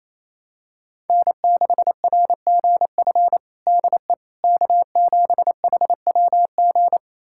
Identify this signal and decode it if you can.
Morse code, heard as a keyed tone, about 28 wpm, 700 Hz. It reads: N6RGF DE K7HWG
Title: USA Callsign DE Callsign - 5 Characters - 28wpm